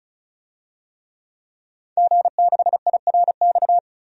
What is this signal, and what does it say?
Morse code, heard as a keyed tone, about 34 wpm, 700 Hz. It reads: G6IRX
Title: DXCC-100 Call Signs - 5 Characters - 35wpm